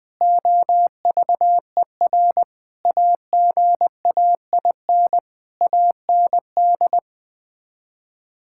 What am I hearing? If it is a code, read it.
Morse code, heard as a keyed tone, about 21 wpm, 700 Hz. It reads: OVER AGAIN AND
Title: Sets of 9 Words - 20wpm